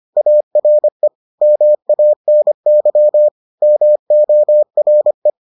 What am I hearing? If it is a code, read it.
Morse code, heard as a keyed tone, about 25 wpm, 600 Hz. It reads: ARE MANY MORE